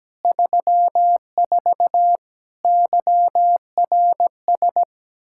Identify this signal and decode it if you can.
Morse code, heard as a keyed tone, about 17 wpm, 700 Hz. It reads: 34 YRS
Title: Practice QSO - Exchange 3 - More Info - 17wpm